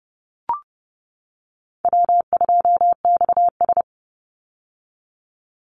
Morse code, heard as a keyed tone, about 30 wpm, 700 Hz. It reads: W2XS
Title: Easy Call Signs - 30wpm